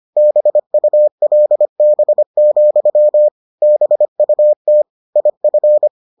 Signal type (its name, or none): Morse code